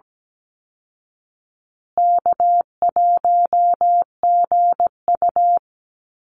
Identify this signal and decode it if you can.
Morse code, heard as a keyed tone, about 17 wpm, 700 Hz. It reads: K1GU
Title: Easy Call Signs Repeated 3x - 17wpm